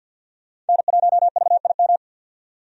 Morse code, heard as a keyed tone, about 51 wpm, 700 Hz. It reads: N1VIM
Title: USA Call Signs - 5 Characters - 50wpm